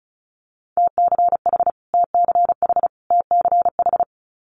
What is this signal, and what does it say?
Morse code, heard as a keyed tone, about 36 wpm, 700 Hz. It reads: TCH TCH TCH
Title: Multi-letter Phonemes Repeated 3x 35wpm